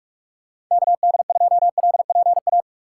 Morse code, heard as a keyed tone, about 46 wpm, 690 Hz. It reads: KD2LPA
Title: USA Call Signs - 6 Characters - 3x Character Spacing - 45wpm